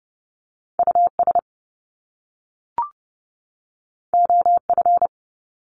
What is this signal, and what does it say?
Morse code, heard as a keyed tone, about 30 wpm, 700 Hz. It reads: US OF
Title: Top Two Letter Combinations in Words 30wpm